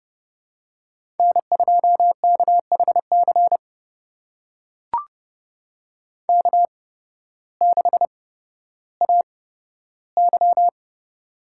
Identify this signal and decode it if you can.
Morse code, heard as a keyed tone, about 30 wpm, spaced as for 14 wpm, 700 Hz. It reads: N2KHC K6AY
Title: USA Call Signs - 8x Character Spacing - 30wpm